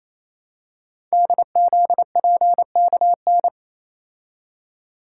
Morse code, heard as a keyed tone, about 27 wpm, 700 Hz. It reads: DZPKN
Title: Code Groups - 5 Letters - 28wpm